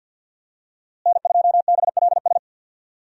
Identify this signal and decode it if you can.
Morse code, heard as a keyed tone, about 50 wpm, 700 Hz. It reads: N2BLS
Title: USA Call Signs - 5 Characters - Repeated 3x - 50wpm